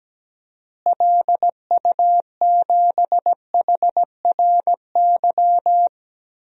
Morse code, heard as a keyed tone, about 17 wpm, 700 Hz. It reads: LU7HRY
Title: DXCC-100 Call Signs - 6 Characters - Repeated 3x - 17wpm